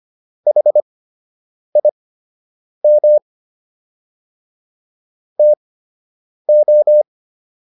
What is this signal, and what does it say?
Morse code, heard as a keyed tone, about 25 wpm, spaced as for 8 wpm, 600 Hz. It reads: HIM TO